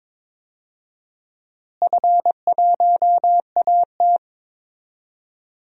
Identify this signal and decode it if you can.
Morse code, heard as a keyed tone, about 21 wpm, 700 Hz. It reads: F1AT
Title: DXCC-100 Call Signs - 4 Characters - Repeated 3x - 22wpm